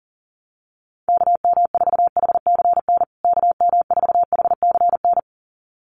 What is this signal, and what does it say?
Morse code, heard as a keyed tone, about 40 wpm, 700 Hz. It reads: KM4HCN KM4HCN
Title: USA Call Signs Repeated 2x 40wpm